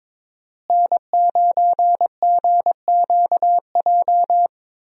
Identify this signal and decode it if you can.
Morse code, heard as a keyed tone, about 22 wpm, 700 Hz. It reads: N9GQJ
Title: USA Call Signs - 5 Characters - Repeated 3x - 22wpm